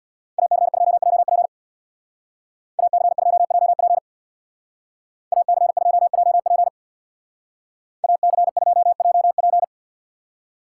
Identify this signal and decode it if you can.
Morse code, heard as a keyed tone, about 58 wpm, 700 Hz. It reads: AC2JP AC2JP AC2JP AC2JP
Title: Speed Racing - USA 2x2 Call Signs 50wpm